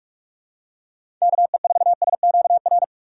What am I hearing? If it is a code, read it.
Morse code, heard as a keyed tone, about 46 wpm, 700 Hz. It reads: KE4SQR
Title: USA Call Signs - 6 Characters - 2x Character Spacing - 45wpm